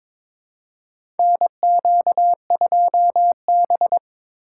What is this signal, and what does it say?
Morse code, heard as a keyed tone, about 22 wpm, 700 Hz. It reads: NQ2B
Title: USA Call Signs - 4 Characters - 2x Character Spacing - 22wpm